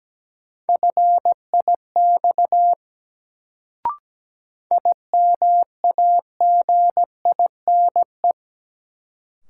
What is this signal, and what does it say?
Morse code, heard as a keyed tone, about 17 wpm, 700 Hz. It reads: FIX IMAGINE
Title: Sentences from Top 1400 Words - New Word Drill - 17wpm